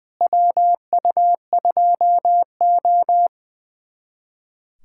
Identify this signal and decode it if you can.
Morse code, heard as a keyed tone, about 21 wpm, 700 Hz. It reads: WU2O
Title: USA Callsign DE Callsign - 4 Characters - 20wpm